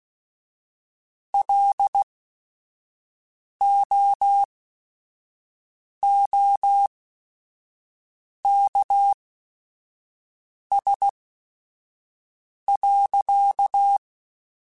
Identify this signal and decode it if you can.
Morse code, heard as a keyed tone, about 16 wpm, spaced as for 5 wpm, 800 Hz. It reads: LOOKS.